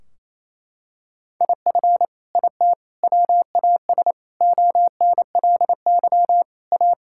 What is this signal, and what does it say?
Morse code, heard as a keyed tone, about 28 wpm, 700 Hz. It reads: IF IT WAS ONLY A
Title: Sentences from Top 1600 Words 28wpm